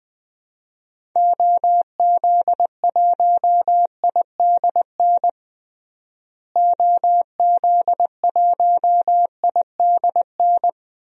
Morse code, heard as a keyed tone, about 20 wpm, 700 Hz. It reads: OZ1IDN OZ1IDN
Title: DXCC-100 Call Signs - 6 Characters - Repeated 3x - 20wpm